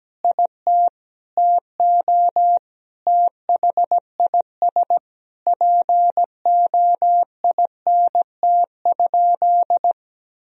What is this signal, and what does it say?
Morse code, heard as a keyed tone, about 17 wpm, 700 Hz. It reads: IT TO THIS POINT?
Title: Sentences from Top 1000 Words - Review Part 1 - 17wpm